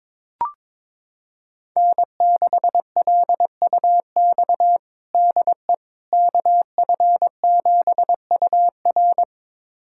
Morse code, heard as a keyed tone, about 22 wpm, 700 Hz. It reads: N6LUX DE KF7UR
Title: USA Callsign DE Callsign - 5 Characters - 22wpm